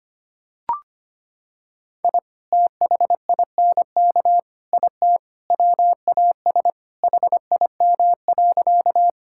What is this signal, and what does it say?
Morse code, heard as a keyed tone, about 25 wpm, 700 Hz. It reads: I THINK IT WAS HIM.
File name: practice / 25wpm / Sentences with 4-Word Phrases - 25wpm